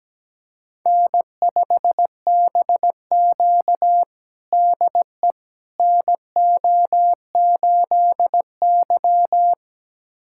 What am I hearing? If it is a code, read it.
Morse code, heard as a keyed tone, about 17 wpm, 700 Hz. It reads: N5BQ DE NO8Y
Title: USA Callsign DE Callsign - 4 Characters - 17wpm